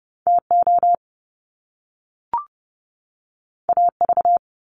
Morse code, heard as a keyed tone, about 30 wpm, 700 Hz. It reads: TO AV
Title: Top Two Letter Combinations in Words 30wpm